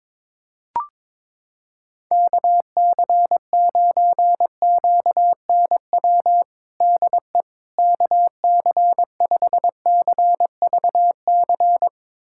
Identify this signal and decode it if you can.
Morse code, heard as a keyed tone, about 22 wpm, 700 Hz. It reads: KC9QNW DE KC5CVC
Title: USA Callsign DE Callsign 22wpm